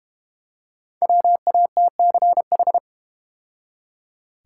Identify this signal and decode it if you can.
Morse code, heard as a keyed tone, about 32 wpm, 700 Hz. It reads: WATCH